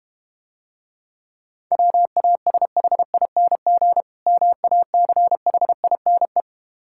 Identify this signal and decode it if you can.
Morse code, heard as a keyed tone, about 32 wpm, 700 Hz. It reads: WASHING MACHINE